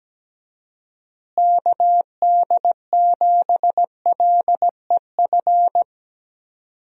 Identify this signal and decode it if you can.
Morse code, heard as a keyed tone, about 17 wpm, 700 Hz. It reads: KD7LEF
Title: USA Call Signs - 6 Characters - Repeated 2x - 17wpm